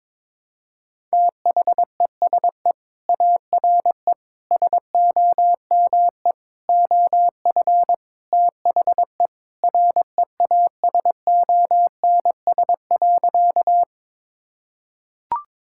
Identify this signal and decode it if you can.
Morse code, heard as a keyed tone, about 22 wpm, 700 Hz. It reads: THESE ARE SOME OF THE REASONS.
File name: practice / 22wpm / Sentences with 4-Word Phrases - 22wpm